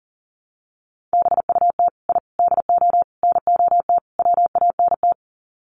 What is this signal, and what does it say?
Morse code, heard as a keyed tone, about 40 wpm, 700 Hz. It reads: BUT I DO NOT WANT
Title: Sets of 5 Words 40wpm